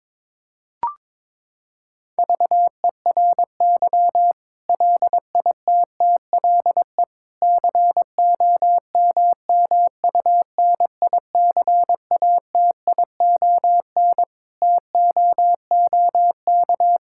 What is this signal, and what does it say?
Morse code, heard as a keyed tone, about 22 wpm, 700 Hz. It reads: VERY LITTLE COMMUNICATION TOOK
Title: Sentences from Top 1000 Words - Review Part 6 - 22wpm